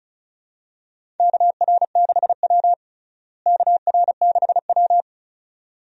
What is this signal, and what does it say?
Morse code, heard as a keyed tone, about 36 wpm, 700 Hz. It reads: KR6W KR6W
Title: USA Call Signs - 4 Characters - Repeated 2x - 35wpm